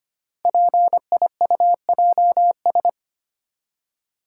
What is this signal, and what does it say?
Morse code, heard as a keyed tone, about 25 wpm, 700 Hz. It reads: PIUJS
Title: Code Groups - 5 Letters - 25wpm